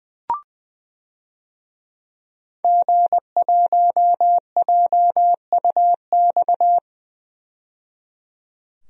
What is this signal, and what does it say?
Morse code, heard as a keyed tone, about 20 wpm, 700 Hz. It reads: G1JUX
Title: DXCC-100 Call Signs - 5 Characters - 20wpm